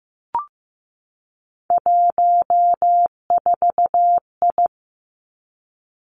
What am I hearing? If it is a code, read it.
Morse code, heard as a keyed tone, about 15 wpm, 700 Hz. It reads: 14I
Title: Three Challenging Letter-Number Combinations 15wpm